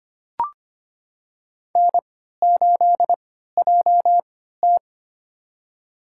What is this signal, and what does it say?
Morse code, heard as a keyed tone, about 25 wpm, spaced as for 14 wpm, 700 Hz. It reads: N8JT
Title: USA Call Signs - 3x Character Spacing - 25wpm